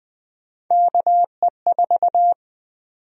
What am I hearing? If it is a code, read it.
Morse code, heard as a keyed tone, about 21 wpm, 700 Hz. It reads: KE4
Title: USA Call Sign Prefixes - Repeated 2x - 20wpm